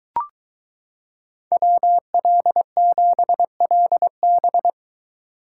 Morse code, heard as a keyed tone, about 22 wpm, 700 Hz. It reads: WL7LB